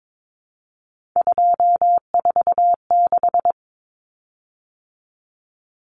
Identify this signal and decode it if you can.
Morse code, heard as a keyed tone, about 22 wpm, 700 Hz. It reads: 246
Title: Common Cut Numbers - 3-Digit 22wpm